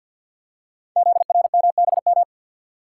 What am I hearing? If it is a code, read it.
Morse code, heard as a keyed tone, about 49 wpm, 700 Hz. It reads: ZRMBM
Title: Code Groups - 5 Letters - 50wpm